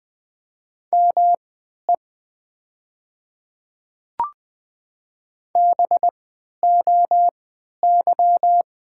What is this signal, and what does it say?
Morse code, heard as a keyed tone, about 20 wpm, spaced as for 11 wpm, 700 Hz. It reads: ME BOY
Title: 3x Character Spacing - Top 200 Words - 20wpm